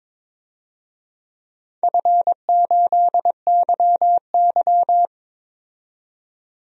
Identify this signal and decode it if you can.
Morse code, heard as a keyed tone, about 22 wpm, 700 Hz. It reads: F8YY
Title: DXCC-100 Call Signs - 4 Characters - 22wpm